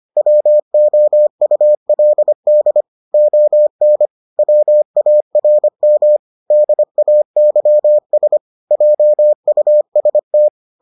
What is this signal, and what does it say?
Morse code, heard as a keyed tone, about 25 wpm, 600 Hz. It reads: WOULD ON WARM DAYS JUST